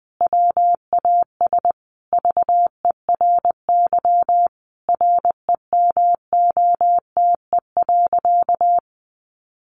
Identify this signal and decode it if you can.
Morse code, heard as a keyed tone, about 20 wpm, 700 Hz. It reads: WAS VERY REMOTE.
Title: Sentences from Top 1300 Words 20wpm